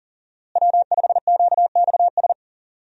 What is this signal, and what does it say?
Morse code, heard as a keyed tone, about 40 wpm, 700 Hz. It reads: W5QXS
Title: USA Call Signs - 5 Characters - Repeated 3x - 40wpm